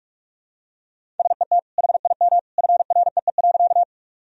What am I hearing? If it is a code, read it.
Morse code, heard as a keyed tone, about 41 wpm, 700 Hz. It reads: SET HIM FREE.
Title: Sentences from Top 1000 Words - Review Part 1 - 45wpm